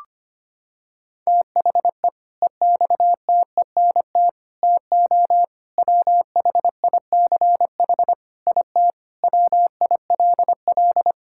Morse code, heard as a keyed tone, about 25 wpm, 700 Hz. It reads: THE EXTENT TO WHICH IT WILL
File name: practice / 25wpm / Sentences with 4-Word Phrases - 25wpm